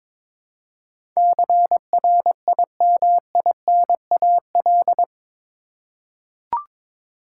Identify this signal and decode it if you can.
Morse code, heard as a keyed tone, about 21 wpm, 700 Hz. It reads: CRIMINAL
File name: practice / 22wpm / Eight Letter Words - 22wpm